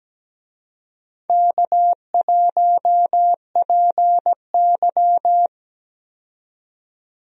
Morse code, heard as a keyed tone, about 17 wpm, 700 Hz. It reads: K1PY
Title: USA Call Signs - 4 Characters - 17wpm